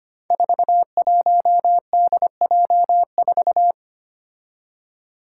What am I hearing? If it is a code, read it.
Morse code, heard as a keyed tone, about 25 wpm, 700 Hz. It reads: 41DJ4
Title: Five Challenging Letter-Number Combinations 25wpm